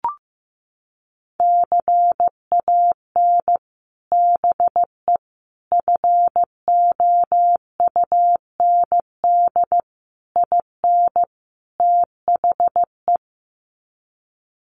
Morse code, heard as a keyed tone, about 15 wpm, 700 Hz. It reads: CAN BE FOUND IN THE
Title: Sets of 5 Words Encore 15wpm